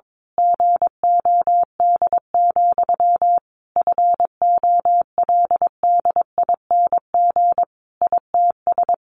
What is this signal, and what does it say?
Morse code, heard as a keyed tone, about 22 wpm, 700 Hz. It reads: GOD, FOLDING ITS
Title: A Sound Of Thunder 22wpm